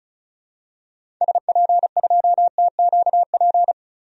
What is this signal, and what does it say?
Morse code, heard as a keyed tone, about 34 wpm, 700 Hz. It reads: SP2TQP
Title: DXCC-100 Call Signs - 6 Characters - 35wpm